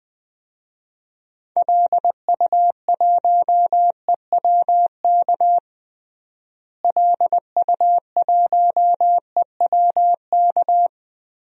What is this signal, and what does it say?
Morse code, heard as a keyed tone, about 20 wpm, 700 Hz. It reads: LU1EWK LU1EWK
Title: DXCC-100 Call Signs - 6 Characters - Repeated 3x - 20wpm